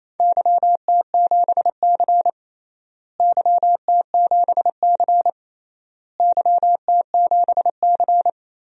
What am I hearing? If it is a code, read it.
Morse code, heard as a keyed tone, about 28 wpm, 700 Hz. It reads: YT7C YT7C YT7C
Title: DXCC-100 Call Signs - 4 Characters - Repeated 3x - 28wpm